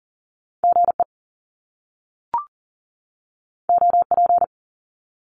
Morse code, heard as a keyed tone, about 38 wpm, 700 Hz. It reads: GE OP
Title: Top Two Letter Combinations in Words 40wpm